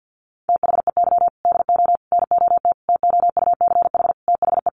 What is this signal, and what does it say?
Morse code, heard as a keyed tone, about 50 wpm, 700 Hz. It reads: THEY DO NOT TOUCH THE
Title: Sentences from Top 1200 Words 50wpm